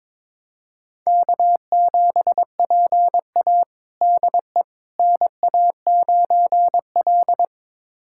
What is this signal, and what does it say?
Morse code, heard as a keyed tone, about 21 wpm, 700 Hz. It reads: K7PA DE NA9L
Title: USA Callsign DE Callsign - 4 Characters - 22wpm